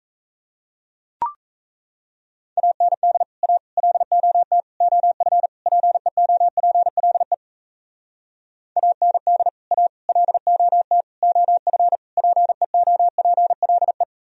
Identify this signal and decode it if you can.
Morse code, heard as a keyed tone, about 33 wpm, 700 Hz. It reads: AND A LOT OF PEOPLE AND A LOT OF PEOPLE
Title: Speed-Racing - Sets of 5 Words 28wpm